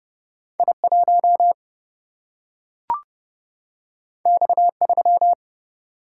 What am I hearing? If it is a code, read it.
Morse code, heard as a keyed tone, about 32 wpm, 700 Hz. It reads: I 1 X 3
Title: Single-Letter Single-Number 30wpm